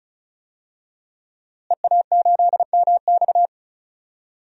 Morse code, heard as a keyed tone, about 34 wpm, 700 Hz. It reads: EA8MX
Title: DXCC-100 Call Signs - 5 Characters - Repeated 3x - 35wpm